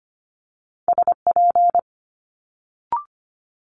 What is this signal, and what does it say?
Morse code, heard as a keyed tone, about 25 wpm, 700 Hz. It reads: SP
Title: Multi-letter Phonemes Quick Fire 25wpm